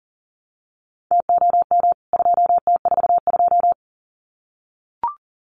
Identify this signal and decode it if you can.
Morse code, heard as a keyed tone, about 40 wpm, 700 Hz. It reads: TOM 2T42
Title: CWT Exchange 40wpm